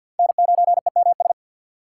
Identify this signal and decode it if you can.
Morse code, heard as a keyed tone, about 47 wpm, 690 Hz. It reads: N9EMS
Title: USA Call Signs - 6x Character Spacing - 50wpm